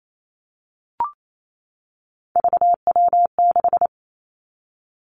Morse code, heard as a keyed tone, about 28 wpm, 700 Hz. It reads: VW6
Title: Three Challenging Letter-Number Combinations 28wpm